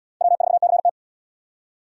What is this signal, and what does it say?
Morse code, heard as a keyed tone, about 70 wpm, 700 Hz. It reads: W4CT